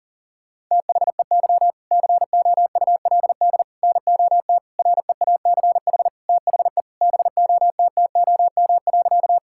Morse code, heard as a keyed tone, about 40 wpm, 700 Hz. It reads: THEY COULD NOT REACH THE BOTTOM.
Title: Sentences from Top 1000 Words - Review Part 6 - 40wpm